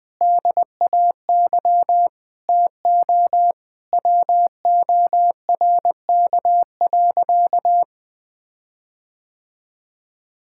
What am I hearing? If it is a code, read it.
Morse code, heard as a keyed tone, about 20 wpm, 700 Hz. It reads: DAY TO WORK.
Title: Sentences with 9-Word Phrases - Spoken Beginning - 20wpm